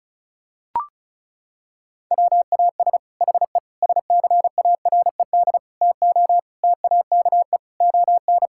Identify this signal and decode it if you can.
Morse code, heard as a keyed tone, about 34 wpm, 700 Hz. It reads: WAS HE SCARED TO TAKE ON
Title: Sentences from Top 1000 Words - Review Part 6 - 35wpm